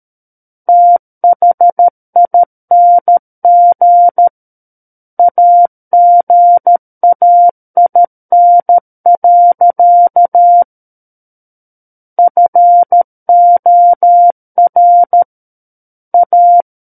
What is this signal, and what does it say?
Morse code, heard as a keyed tone, about 13 wpm, 700 Hz. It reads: THING AGAIN. FOR A
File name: Book One / 04-Chapter 4 - 13 WPM